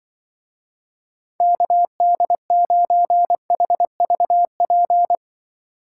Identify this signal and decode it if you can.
Morse code, heard as a keyed tone, about 23 wpm, 700 Hz. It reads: KD9HVP
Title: Speed Racing - USA 2x3 Call Signs 20wpm